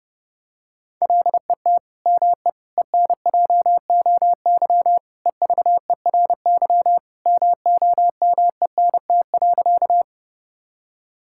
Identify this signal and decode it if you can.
Morse code, heard as a keyed tone, about 30 wpm, 700 Hz. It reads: LET ME ENJOY EVERY MOMENT.
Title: Sentences from Top 1000 Words - Review Part 5 - 30wpm